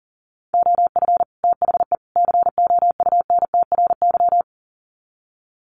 Morse code, heard as a keyed tone, about 40 wpm, 700 Hz. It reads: OF THE COUNTRY
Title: Sets of 3 Words 40wpm